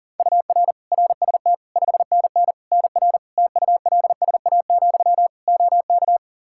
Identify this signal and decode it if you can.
Morse code, heard as a keyed tone, about 37 wpm, 700 Hz. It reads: UR RST 5NN NR TULSA, OK
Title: Practice QSO - Exchange 1 - Beginning - 40wpm